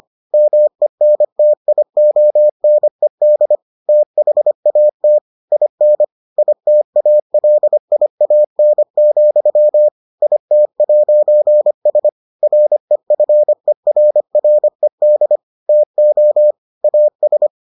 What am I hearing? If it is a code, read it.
Morse code, heard as a keyed tone, about 25 wpm, 600 Hz. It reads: MENTIONED THAT IN ITALIAN, IT'S REFERRED TO AS